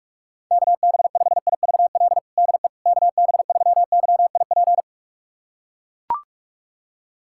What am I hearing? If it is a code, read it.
Morse code, heard as a keyed tone, about 41 wpm, 700 Hz. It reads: KB5IVL DE KB3YIP
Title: USA Callsign DE Callsign - 6 Characters - 45wpm